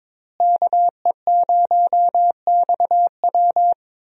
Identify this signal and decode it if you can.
Morse code, heard as a keyed tone, about 22 wpm, 700 Hz. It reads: KE0XW
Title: USA Call Signs - 5 Characters - Repeated 2x - 22wpm